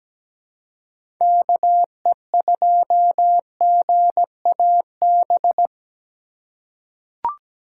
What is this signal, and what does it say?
Morse code, heard as a keyed tone, about 17 wpm, 700 Hz. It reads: KE2GAB
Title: USA Call Signs - 6 Characters - 2x Character Spacing - 17wpm